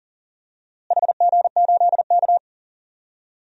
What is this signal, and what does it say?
Morse code, heard as a keyed tone, about 37 wpm, 700 Hz. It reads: HG8K
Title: DXCC-100 Call Signs - 4 Characters - Repeated 3x - 40wpm